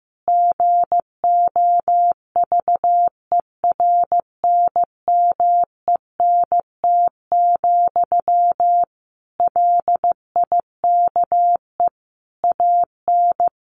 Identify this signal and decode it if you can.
Morse code, heard as a keyed tone, about 14 wpm, 700 Hz. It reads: GOVERNMENT, LIKE AN
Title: Sentences from Top 1400 Words 15wpm